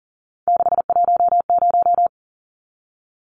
Morse code, heard as a keyed tone, about 40 wpm, 700 Hz. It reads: 610
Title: Numbers - 3-Digit 40wpm